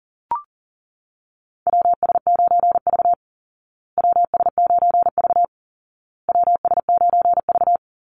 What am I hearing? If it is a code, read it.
Morse code, heard as a keyed tone, about 40 wpm, 700 Hz. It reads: WS9V WS9V WS9V
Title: Call Signs Repeated 2x - 40wpm